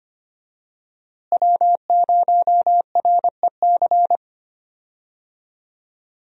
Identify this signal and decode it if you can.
Morse code, heard as a keyed tone, about 25 wpm, 700 Hz. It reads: W0REC
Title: USA Call Signs - 5 Characters - Repeated 3x - 25wpm